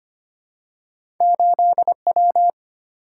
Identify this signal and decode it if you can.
Morse code, heard as a keyed tone, about 25 wpm, 700 Hz. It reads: 8W
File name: Two Letter-Number Combinations - 25wpm